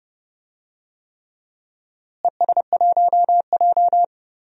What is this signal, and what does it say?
Morse code, heard as a keyed tone, about 31 wpm, 700 Hz. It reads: ES1J
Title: DXCC-100 Call Signs - 4 Characters - Repeated 3x - 30wpm